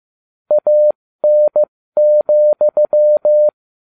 Morse code, heard as a keyed tone, about 15 wpm, spaced as for 13 wpm, 600 Hz. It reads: AN,